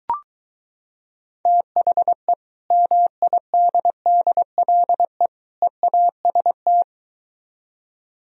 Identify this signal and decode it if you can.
Morse code, heard as a keyed tone, about 23 wpm, 700 Hz. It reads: THE MIDDLE EAST